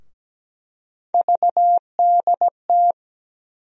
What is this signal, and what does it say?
Morse code, heard as a keyed tone, about 17 wpm, 700 Hz. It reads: VDT
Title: USA Call Sign Postfixes - 3 Letters 17wpm